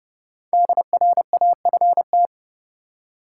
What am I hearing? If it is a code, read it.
Morse code, heard as a keyed tone, about 30 wpm, 700 Hz. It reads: DRAFT
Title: Five Letter Words - 30wpm